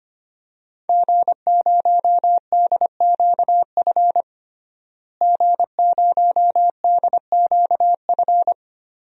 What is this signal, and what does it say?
Morse code, heard as a keyed tone, about 25 wpm, 700 Hz. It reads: G0DQF G0DQF
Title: DXCC-100 Call Signs - 5 Characters - Repeated 3x - 25wpm